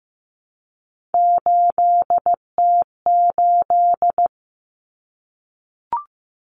Morse code, heard as a keyed tone, about 15 wpm, 700 Hz. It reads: 8T8
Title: Common Cut Numbers - 3-Digit 15wpm